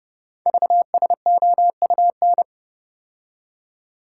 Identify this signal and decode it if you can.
Morse code, heard as a keyed tone, about 28 wpm, 690 Hz. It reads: VSOUN
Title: Code Groups - 5 Letters - 30wpm